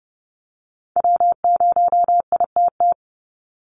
Morse code, heard as a keyed tone, about 30 wpm, 700 Hz. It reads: W0ITT
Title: Call Signs Repeated 2x - 30wpm